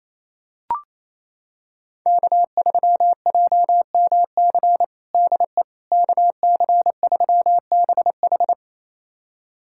Morse code, heard as a keyed tone, about 27 wpm, 700 Hz. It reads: K3JMC DE KC3BH
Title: USA Callsign DE Callsign - 5 Characters - 28wpm